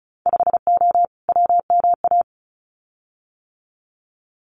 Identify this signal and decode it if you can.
Morse code, heard as a keyed tone, about 36 wpm, 700 Hz. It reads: 5O WMA
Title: Winter Field Day Exchanges Encore 35wpm